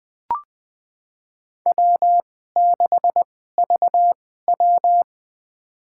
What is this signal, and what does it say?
Morse code, heard as a keyed tone, about 20 wpm, spaced as for 15 wpm, 700 Hz. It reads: W6VW
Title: USA Call Signs - 2x Character Spacing - 20wpm